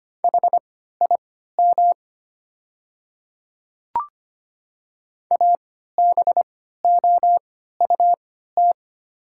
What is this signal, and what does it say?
Morse code, heard as a keyed tone, about 25 wpm, spaced as for 14 wpm, 700 Hz. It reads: HIM ABOUT
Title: 3x Character Spacing - Top 100 Words - 25wpm